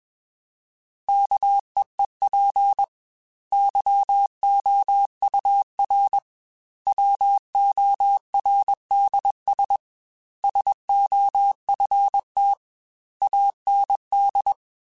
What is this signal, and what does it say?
Morse code, heard as a keyed tone, about 21 wpm, 800 Hz. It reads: KEEP YOUR WORDS SOFT AND